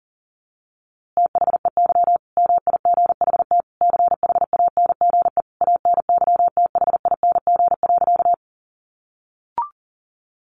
Morse code, heard as a keyed tone, about 40 wpm, 700 Hz. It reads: THEY MIGHT CHANGE ANYTHING.
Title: A Sound Of Thunder 40wpm